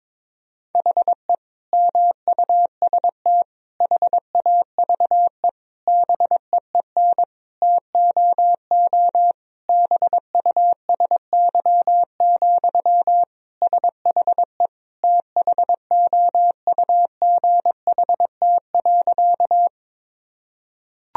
Morse code, heard as a keyed tone, about 22 wpm, 700 Hz. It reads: HE MUST HAVE BEEN TOO BUSY, SHE THOUGHT.
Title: Sentences from Top 1000 Words - Review Part 3 - 22wpm